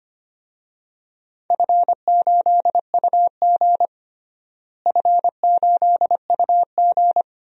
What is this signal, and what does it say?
Morse code, heard as a keyed tone, about 25 wpm, 700 Hz. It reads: F8UG F8UG
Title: DXCC-100 Call Signs - 4 Characters - Repeated 2x - 25wpm